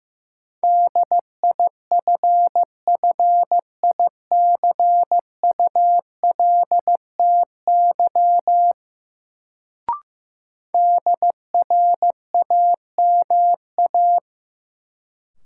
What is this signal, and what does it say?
Morse code, heard as a keyed tone, about 16 wpm, 700 Hz. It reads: DIFFICULTY DRAMA
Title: Sentences from Top 1500 Words - New Word Drill - 15wpm